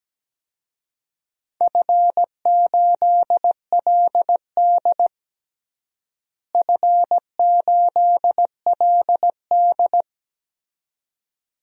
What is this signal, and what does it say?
Morse code, heard as a keyed tone, about 17 wpm, 700 Hz. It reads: F8LD F8LD
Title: DXCC-100 Call Signs - 4 Characters - Repeated 2x - 17wpm